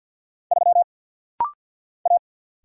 Morse code, heard as a keyed tone, about 50 wpm, 700 Hz. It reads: 3 A
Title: Single Letter-Number - Rapid-Fire 50wpm